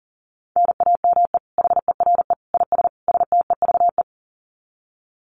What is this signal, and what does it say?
Morse code, heard as a keyed tone, about 40 wpm, 700 Hz. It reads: NAME HERE IS STEVE
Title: Common QSO Phrases 40wpm